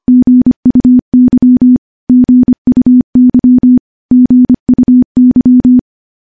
Morse code, heard as a keyed tone, about 25 wpm, 250 Hz. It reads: GUY GUY GUY
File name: morse-training-25WPM